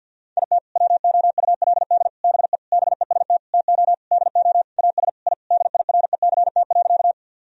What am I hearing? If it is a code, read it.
Morse code, heard as a keyed tone, about 47 wpm, 700 Hz. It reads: IT WOULD BE BEST TO DO AS I DIRECT.
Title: Sentences from Top 1000 Words - Review Part 2 - 50wpm